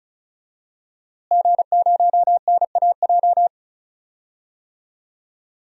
Morse code, heard as a keyed tone, about 36 wpm, 700 Hz. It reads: G0NAJ
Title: DXCC-100 Call Signs - 5 Characters - Repeated 2x - 35wpm